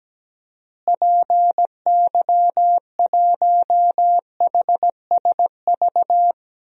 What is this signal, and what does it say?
Morse code, heard as a keyed tone, about 17 wpm, 700 Hz. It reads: PY1HSV
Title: DXCC-100 Call Signs - 6 Characters - Repeated 2x - 17wpm